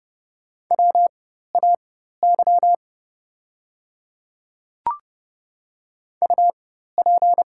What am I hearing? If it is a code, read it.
Morse code, heard as a keyed tone, about 31 wpm, spaced as for 14 wpm, 690 Hz. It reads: WAY UP
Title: 4x Character Spacing - Top 100 Words - 30wpm